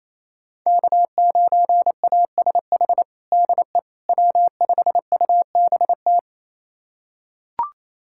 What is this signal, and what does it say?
Morse code, heard as a keyed tone, about 28 wpm, 700 Hz. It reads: K9ASH DE W5UBT
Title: USA Callsign DE Callsign - 5 Characters - 28wpm